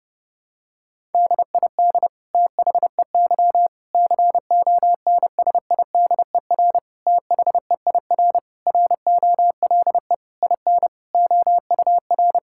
Morse code, heard as a keyed tone, about 30 wpm, 700 Hz. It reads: DID THEY CONSIDER THEIR ROLE IN OUR
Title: Sentences from Top 1000 Words - Review Part 6 - 30wpm